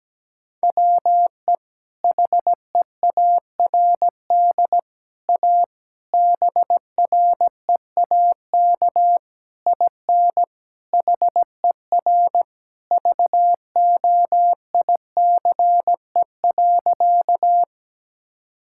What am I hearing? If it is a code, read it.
Morse code, heard as a keyed tone, about 17 wpm, 700 Hz. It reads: WE HEARD A BREAK IN HER VOICE.
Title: Sentences from Top 1000 Words - Review Part 6 - 17wpm